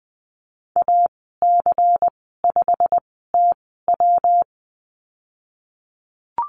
Morse code, heard as a keyed tone, about 20 wpm, spaced as for 15 wpm, 700 Hz. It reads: AC5TW
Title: 128 Lesson 21g - Number 6 - Call Signs 20wpm